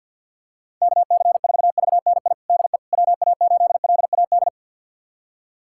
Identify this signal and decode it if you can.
Morse code, heard as a keyed tone, about 46 wpm, 700 Hz. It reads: KC4VNI DE WA8LAD
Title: USA Callsign DE Callsign 50wpm